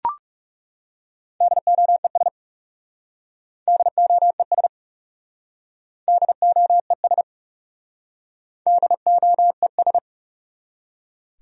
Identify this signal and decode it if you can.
Morse code, heard as a keyed tone, about 40 wpm, 700 Hz. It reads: DOES DOES DOES DOES